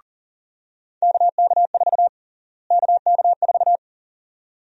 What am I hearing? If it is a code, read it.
Morse code, heard as a keyed tone, about 40 wpm, 700 Hz. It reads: KK4 KK4
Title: USA Call Sign Prefixes - Repeated 2x - 40wpm